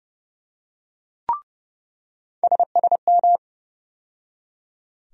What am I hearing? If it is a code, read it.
Morse code, heard as a keyed tone, about 30 wpm, 700 Hz. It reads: SSM
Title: USA Call Sign Postfixes - 3 Letters 30wpm